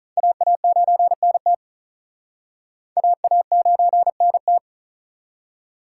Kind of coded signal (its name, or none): Morse code